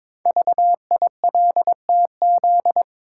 Morse code, heard as a keyed tone, about 22 wpm, 700 Hz. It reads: VILTZ
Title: Code Groups - 5 Letters - 22wpm